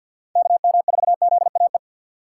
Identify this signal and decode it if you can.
Morse code, heard as a keyed tone, about 53 wpm, 700 Hz. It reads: KM4ZAE
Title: USA Call Signs - 6 Characters - 3x Character Spacing - 50wpm